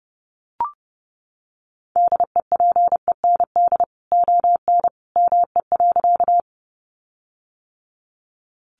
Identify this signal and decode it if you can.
Morse code, heard as a keyed tone, about 30 wpm, 700 Hz. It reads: DEPEND ON ME.
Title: Sentences from Top 1300 Words 30wpm